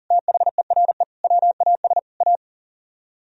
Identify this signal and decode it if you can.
Morse code, heard as a keyed tone, about 50 wpm, 700 Hz. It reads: THERE WAS A